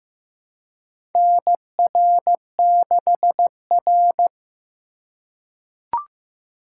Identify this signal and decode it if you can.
Morse code, heard as a keyed tone, about 15 wpm, 700 Hz. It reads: NR6R
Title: USA Call Signs - 4x Character Spacing - 15wpm